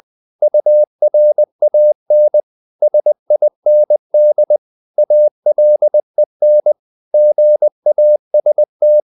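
Morse code, heard as a keyed tone, about 20 wpm, 600 Hz. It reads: URAN SIND ALEN GAST
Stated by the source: Morse code practice of short Danish words at 20 wpm, from CW med Gnister.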